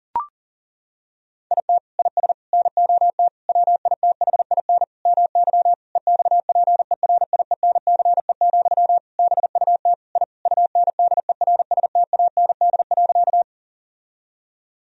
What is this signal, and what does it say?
Morse code, heard as a keyed tone, about 40 wpm, 700 Hz. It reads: IT IS NOT WITHIN MY EXPERIENCE, BUT I UNDERSTAND.
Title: Sentences from Top 1000 Words - Review Part 4 - 40wpm